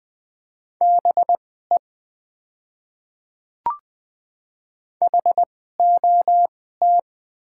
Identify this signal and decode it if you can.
Morse code, heard as a keyed tone, about 20 wpm, spaced as for 14 wpm, 700 Hz. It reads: BE HOT
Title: 2x Character Spacing - Top 100 Words - 20wpm